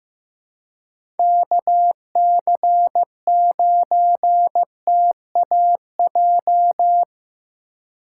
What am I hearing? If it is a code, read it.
Morse code, heard as a keyed tone, about 15 wpm, 700 Hz. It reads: KC9TAJ
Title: USA Call Signs - 3x Character Spacing - 15wpm